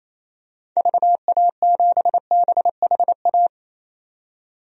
Morse code, heard as a keyed tone, about 28 wpm, 700 Hz. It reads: VA7BHA
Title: DXCC-100 Call Signs - 6 Characters - 28wpm